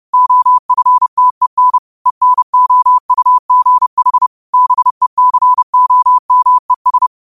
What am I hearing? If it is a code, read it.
Morse code, heard as a keyed tone, about 30 wpm, 1000 Hz. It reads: OFTEN ENOUGH BECOMES
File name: qotdfiles / 30wpm